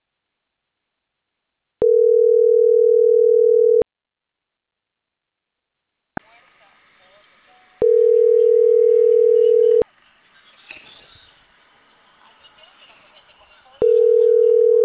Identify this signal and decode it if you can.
Ringback tone (North America), 440 + 480 Hz, with speech.
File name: CENTRAL CAEA / out agcaea02